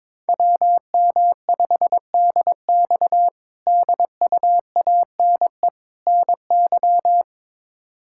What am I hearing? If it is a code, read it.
Morse code, heard as a keyed tone, about 22 wpm, 700 Hz. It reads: WM5DX DUANE NY
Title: K1USN SST QSO Practice - 22wpm